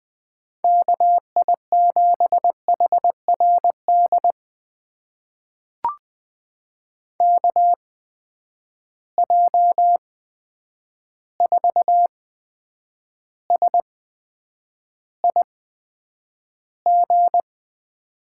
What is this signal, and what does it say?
Morse code, heard as a keyed tone, about 20 wpm, spaced as for 9 wpm, 700 Hz. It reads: KI7HRD KJ4SIG
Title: USA Call Signs - 6 Characters - 8x Character Spacing - 20wpm